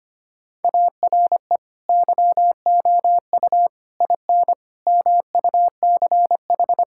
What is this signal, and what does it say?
Morse code, heard as a keyed tone, about 25 wpm, 700 Hz. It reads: ARE YOU IN MUCH
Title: Sentences from Top 1600 Words 25wpm